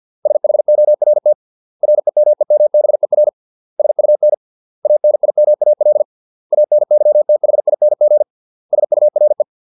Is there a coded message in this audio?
Yes: Morse code from 0.3 to 9.4 s, heard as a keyed tone, about 50 wpm, 600 Hz. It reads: SHORT REMEMBER SUN ANIMAL ANYTHING SURE